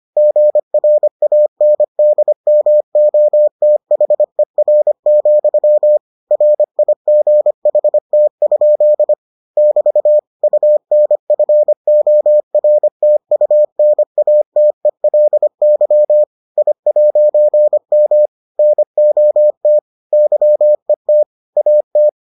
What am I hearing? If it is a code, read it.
Morse code, heard as a keyed tone, about 25 wpm, 600 Hz. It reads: GRANDMOTHER, RIGHT? <BT> UNFORTUNATELY I'M NOT YET AT